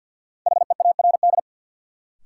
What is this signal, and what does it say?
Morse code, heard as a keyed tone, about 46 wpm, 700 Hz. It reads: HEARD
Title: Sentences from Top 400 Words - New Word Drill - 50wpm